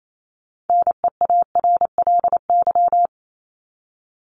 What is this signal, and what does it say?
Morse code, heard as a keyed tone, about 28 wpm, 700 Hz. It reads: NEARLY
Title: LY Suffix - Top 100 Words with Base Words 28wpm